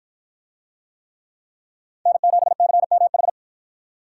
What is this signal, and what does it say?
Morse code, heard as a keyed tone, about 53 wpm, 700 Hz. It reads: N7CMH